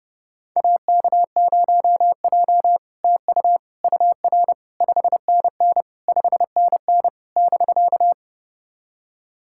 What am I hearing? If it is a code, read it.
Morse code, heard as a keyed tone, about 30 wpm, 700 Hz. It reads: AK0J TU UR 5NN 5NN <BK>
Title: SOTA QSO Practice 30wpm